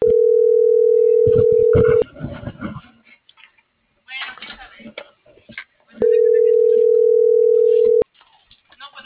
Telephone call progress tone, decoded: ringback tone

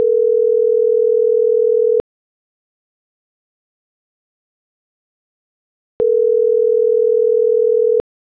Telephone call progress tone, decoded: ringback tone

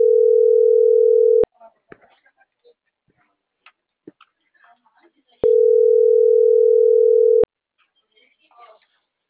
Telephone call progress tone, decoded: ringback tone

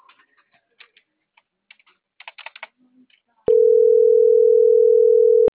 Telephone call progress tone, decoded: ringback tone

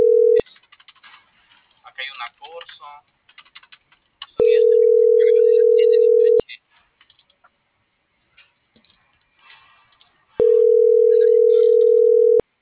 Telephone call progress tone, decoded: ringback tone

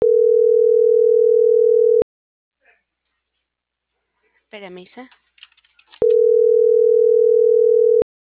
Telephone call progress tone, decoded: ringback tone

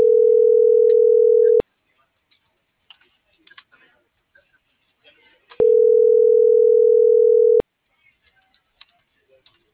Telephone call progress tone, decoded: ringback tone

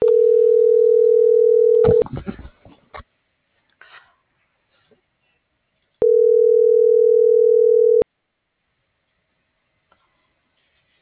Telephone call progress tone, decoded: ringback tone

